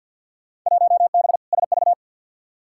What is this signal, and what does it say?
Morse code, heard as a keyed tone, about 51 wpm, 700 Hz. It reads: 1B SV
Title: Field Day Exchanges - 50wpm